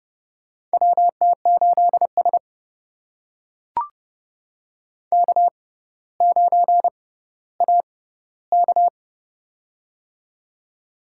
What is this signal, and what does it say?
Morse code, heard as a keyed tone, about 31 wpm, spaced as for 16 wpm, 700 Hz. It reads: WT8S K9AK
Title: USA Call Signs - 4 Characters - 6x Character Spacing - 30wpm